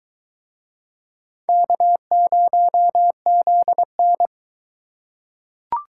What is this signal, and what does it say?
Morse code, heard as a keyed tone, about 22 wpm, 700 Hz. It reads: K0ZN